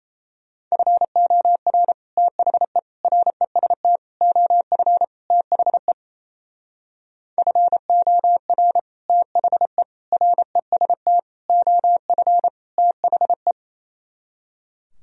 Morse code, heard as a keyed tone, about 34 wpm, 700 Hz. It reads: FOR THE REST OF THE FOR THE REST OF THE
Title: Speed-Racing - Sets of 5 Words 28wpm